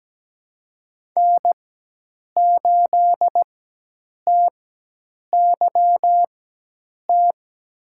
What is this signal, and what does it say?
Morse code, heard as a keyed tone, about 17 wpm, spaced as for 8 wpm, 700 Hz. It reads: N8TYT
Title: USA Call Signs - 5 Characters - 4x Character Spacing - 17wpm